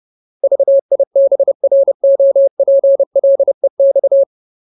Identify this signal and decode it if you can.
Morse code, heard as a keyed tone, about 30 wpm, 550 Hz. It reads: VIBROPLEX
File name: Ultimate Word List / 30WPM Ultimate